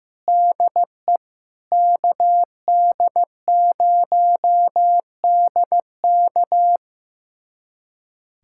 Morse code, heard as a keyed tone, about 15 wpm, 700 Hz. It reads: DE KD0DK
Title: USA Callsign DE Callsign - 5 Characters - 15wpm